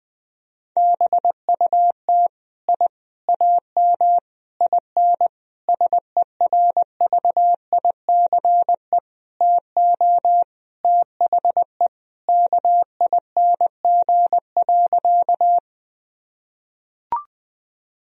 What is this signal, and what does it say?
Morse code, heard as a keyed tone, about 20 wpm, 700 Hz. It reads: BUT I AM IN SERVICE TO THE KING.
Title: Sentences from Top 1000 Words - Review Part 4 - 20wpm